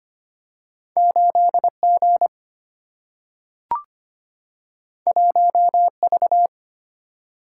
Morse code, heard as a keyed tone, about 25 wpm, 700 Hz. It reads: 8 G 1 V
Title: Single-Letter Single-Number 25wpm